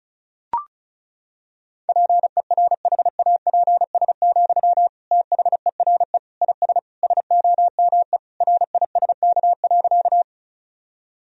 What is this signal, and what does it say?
Morse code, heard as a keyed tone, about 36 wpm, 700 Hz. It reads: PERHAPS, THERE IS SOME RISK.
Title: Sentences from Top 1000 Words - Review Part 4 - 35wpm